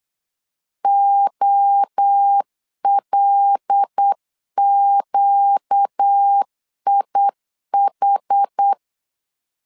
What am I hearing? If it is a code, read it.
Morse code, heard as a keyed tone, about 8 wpm, 800 Hz. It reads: OLQIH
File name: LEZIONI / 7 GF9Z / 7 G 8wpm 3118